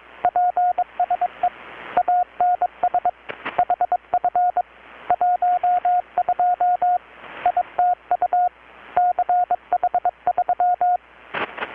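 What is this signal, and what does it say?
Morse code, heard as a keyed tone, about 21 wpm, 720 Hz, with radio noise. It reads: PSE ANS HF 12 ITU CH3
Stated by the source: Morse from KPH's Night of Nights XX maritime radiotelegraphy event, received in Chiapas, Mexico about 3,800 km away on a Kenwood R-600 with a grounded 10 m long-wire antenna.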